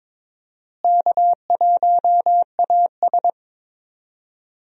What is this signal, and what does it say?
Morse code, heard as a keyed tone, about 22 wpm, 700 Hz. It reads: K1AS
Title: USA Call Signs - 4 Characters - Repeated 3x - 22wpm